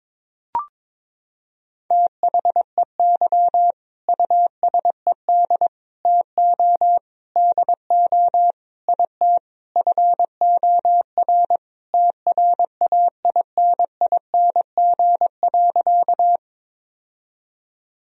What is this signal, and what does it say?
Morse code, heard as a keyed tone, about 22 wpm, 700 Hz. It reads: THEY USED TO DO IT FOR TRAINING.
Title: Sentences from Top 1000 Words - Review Part 4 - 22wpm